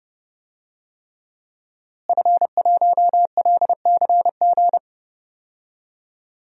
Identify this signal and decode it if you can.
Morse code, heard as a keyed tone, about 30 wpm, 700 Hz. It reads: F1LCG
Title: DXCC-100 Call Signs - 5 Characters - Repeated 3x - 30wpm